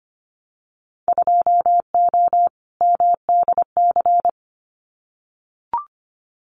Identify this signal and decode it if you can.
Morse code, heard as a keyed tone, about 25 wpm, 700 Hz. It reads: 2O MDC
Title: Winter Field Day Exchanges 25wpm